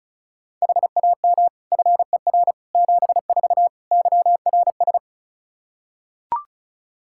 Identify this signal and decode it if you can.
Morse code, heard as a keyed tone, about 36 wpm, 700 Hz. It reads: HAM FER 74 YRS
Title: Practice QSO - Exchange 3 - More Info - 35wpm